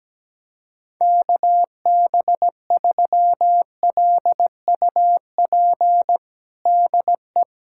Morse code, heard as a keyed tone, about 17 wpm, 700 Hz. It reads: KB3LUP DE
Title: USA Callsign DE Callsign 17wpm